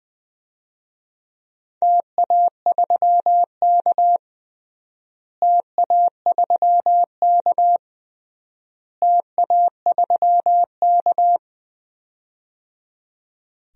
Morse code, heard as a keyed tone, about 20 wpm, 700 Hz. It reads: TA3K TA3K TA3K
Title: DXCC-100 Call Signs - 4 Characters - Repeated 3x - 20wpm